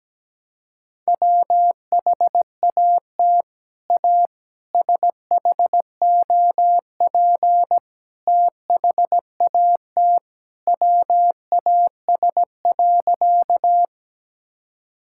Morse code, heard as a keyed tone, about 17 wpm, 700 Hz. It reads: WHAT A SHOP THAT WAS.
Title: Sentences from Top 1600 Words 17wpm